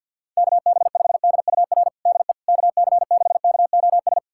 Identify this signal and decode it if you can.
Morse code, heard as a keyed tone, about 46 wpm, 700 Hz. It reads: KB5DUR DE KC6KOS
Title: USA Callsign DE Callsign - 6 Characters - 50wpm